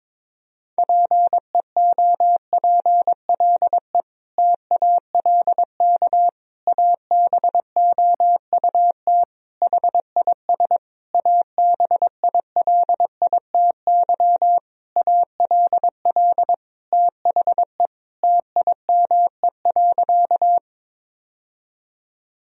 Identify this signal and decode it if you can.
Morse code, heard as a keyed tone, about 22 wpm, 700 Hz. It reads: PEOPLE TALK ABOUT HIS ABILITY ALL THE TIME.
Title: Sentences from Top 1000 Words - Review Part 3 - 22wpm